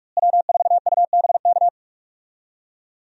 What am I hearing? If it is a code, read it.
Morse code, heard as a keyed tone, about 41 wpm, 700 Hz. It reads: W4UBK
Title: USA Call Signs - 5 Characters - Repeated 3x - 45wpm